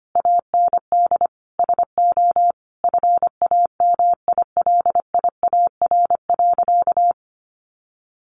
Morse code, heard as a keyed tone, about 25 wpm, 700 Hz. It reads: AND SO FAMILIAR.
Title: Sentences from Top 1000 Words 25wpm